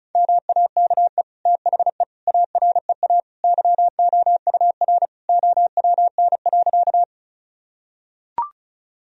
Morse code, heard as a keyed tone, about 36 wpm, 700 Hz. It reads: MAKE THE AREA YOUR OWN.
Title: Sentences from Top 1000 Words - Review Part 2 - 35wpm